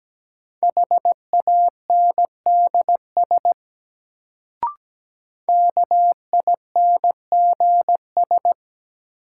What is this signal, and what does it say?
Morse code, heard as a keyed tone, about 17 wpm, 700 Hz. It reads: HANDS KINGS
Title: Sentences from Top 1000 Words - Review Part 6 - New Word Drill - 17wpm